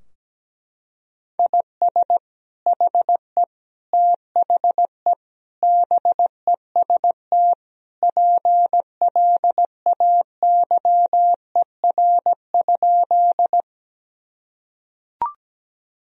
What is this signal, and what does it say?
Morse code, heard as a keyed tone, about 17 wpm, 700 Hz. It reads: IS HE THE BEST PLAYER?
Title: Sentences from Top 1000 Words - Review Part 4 - 17wpm